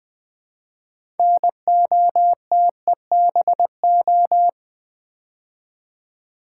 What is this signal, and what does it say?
Morse code, heard as a keyed tone, about 20 wpm, 700 Hz. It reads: NOTEBO
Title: Build A Word - 3 Letters At A Time - 8-Letter Words - 20wpm